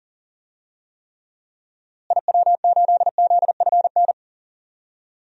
Morse code, heard as a keyed tone, about 40 wpm, 700 Hz. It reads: IW8ZFN
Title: DXCC-100 Call Signs - 6 Characters - Repeated 3x - 40wpm